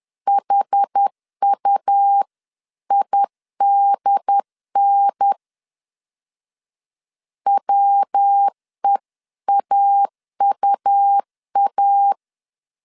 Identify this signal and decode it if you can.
Morse code, heard as a keyed tone, about 10 wpm, 800 Hz. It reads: HUIDN WEAUA
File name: CW / DET / 1 3 1500C S C 10wpm 3335